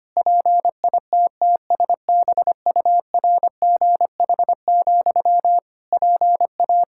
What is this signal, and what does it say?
Morse code, heard as a keyed tone, about 25 wpm, 700 Hz. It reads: PITTSBURGH, PA
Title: Practice QSO - Exchange 1 - Beginning - 25wpm